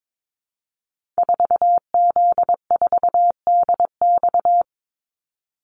Morse code, heard as a keyed tone, about 22 wpm, 690 Hz. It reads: 4Z4DX
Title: Call Signs Repeated 2x - 22wpm